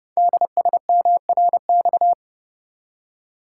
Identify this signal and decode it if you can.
Morse code, heard as a keyed tone, about 30 wpm, 700 Hz. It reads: DSMRX
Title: Code Groups - 5 Letters - 30wpm